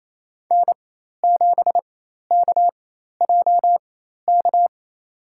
Morse code, heard as a keyed tone, about 28 wpm, spaced as for 13 wpm, 700 Hz. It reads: N7KJK
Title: USA Call Signs - 5 Characters - 4x Character Spacing - 28wpm